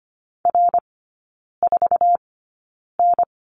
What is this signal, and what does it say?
Morse code, heard as a keyed tone, about 25 wpm, spaced as for 9 wpm, 690 Hz. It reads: R4N
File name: Lesson 15e - Number 5 - Three Characters 25wpm